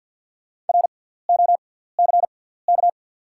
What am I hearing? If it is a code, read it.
Morse code, heard as a keyed tone, about 48 wpm, spaced as for 17 wpm, 690 Hz. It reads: ROCK
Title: 6x Character Spacing - Top 500 Words - 50wpm